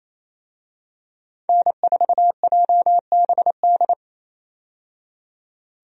Morse code, heard as a keyed tone, about 27 wpm, 700 Hz. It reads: N4JBD
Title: USA Call Signs - 5 Characters - 28wpm